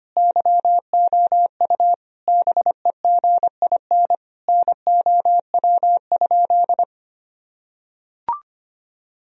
Morse code, heard as a keyed tone, about 25 wpm, 700 Hz. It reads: YOU BEGIN NOW?
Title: Sentences from Top 1000 Words - Review Part 1 - 25wpm